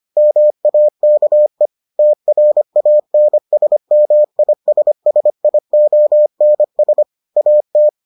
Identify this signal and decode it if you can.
Morse code, heard as a keyed tone, about 25 wpm, 600 Hz. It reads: MAKE TRANSMISSIONS AT